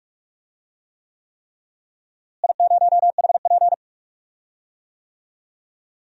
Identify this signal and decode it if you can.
Morse code, heard as a keyed tone, about 45 wpm, 700 Hz. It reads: I0HP
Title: DXCC-100 Call Signs - 4 Characters - Repeated 3x - 45wpm